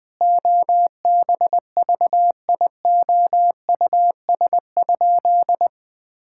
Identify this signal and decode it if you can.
Morse code, heard as a keyed tone, about 20 wpm, 700 Hz. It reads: OBVIOUS?
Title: Sentences from Top 1600 Words 20wpm